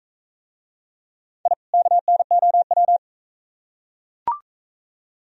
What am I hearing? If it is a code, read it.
Morse code, heard as a keyed tone, about 38 wpm, 700 Hz. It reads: I KNOW